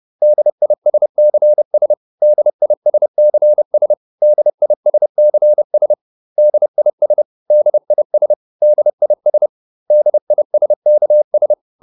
Morse code, heard as a keyed tone, about 30 wpm, 600 Hz. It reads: DISCS DISCS DISCS DIS DIS DIS DISKS